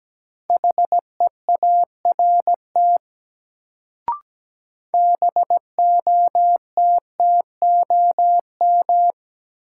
Morse code, heard as a keyed tone, about 17 wpm, 700 Hz. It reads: HEART BOTTOM
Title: Sets of 3 Words - New Word Drill - 17wpm